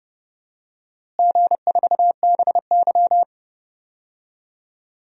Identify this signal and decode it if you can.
Morse code, heard as a keyed tone, about 32 wpm, 700 Hz. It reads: G4BY
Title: DXCC-100 Call Signs - 4 Characters - Repeated 3x - 30wpm